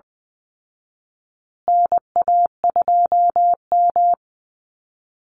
Morse code, heard as a keyed tone, about 20 wpm, 700 Hz. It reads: NA2M
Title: Call Signs Repeated 2x - 20wpm